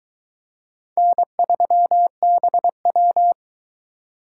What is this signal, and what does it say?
Morse code, heard as a keyed tone, about 23 wpm, 700 Hz. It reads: N3BW